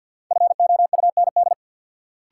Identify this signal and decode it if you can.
Morse code, heard as a keyed tone, about 47 wpm, 700 Hz. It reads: FOUND
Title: Sentences from Top 300 Words - New Word Drill - 50wpm